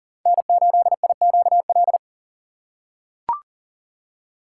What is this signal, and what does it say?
Morse code, heard as a keyed tone, about 40 wpm, 700 Hz. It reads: N8IQL
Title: USA Call Signs - 5 Characters - 4x Character Spacing - 40wpm